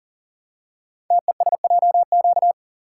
Morse code, heard as a keyed tone, about 39 wpm, 700 Hz. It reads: TESJQ
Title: Code Groups - 5 Letters - 40wpm